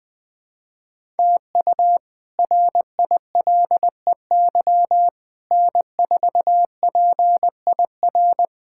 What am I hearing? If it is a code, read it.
Morse code, heard as a keyed tone, about 20 wpm, 700 Hz. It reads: TU RILEY N4PIR
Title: K1USN SST QSO Practice - 20wpm